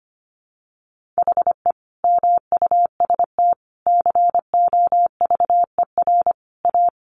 Morse code, heard as a keyed tone, about 25 wpm, 700 Hz. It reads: HE MUST COVER A
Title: Sentences from Top 500 Words 25wpm